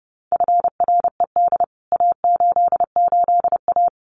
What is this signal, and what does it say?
Morse code, heard as a keyed tone, about 30 wpm, 700 Hz. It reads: FRED A88A
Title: CWT Exchange 30wpm